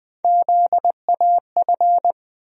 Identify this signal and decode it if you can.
Morse code, heard as a keyed tone, about 20 wpm, 700 Hz. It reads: ZAF
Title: USA Call Sign Postfixes - 3 Letters 20wpm